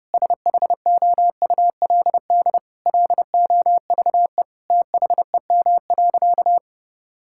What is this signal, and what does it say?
Morse code, heard as a keyed tone, about 30 wpm, 700 Hz. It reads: SHOULD LOVE THEM.
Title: Sentences from Top 1600 Words 30wpm